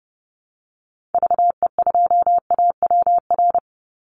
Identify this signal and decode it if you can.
Morse code, heard as a keyed tone, about 30 wpm, 700 Hz. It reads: VE2AWR
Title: Call Signs Repeated 3x - 30wpm